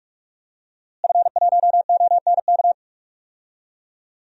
Morse code, heard as a keyed tone, about 46 wpm, 700 Hz. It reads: F1ONK
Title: DXCC-100 Call Signs - 5 Characters - Repeated 2x - 45wpm